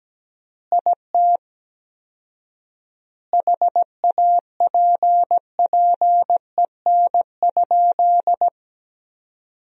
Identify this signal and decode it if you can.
Morse code, heard as a keyed tone, about 17 wpm, 700 Hz. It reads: IT HAPPEN?
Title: 4x Word-Spacing - Sentences from Top 400 Words - 17wpm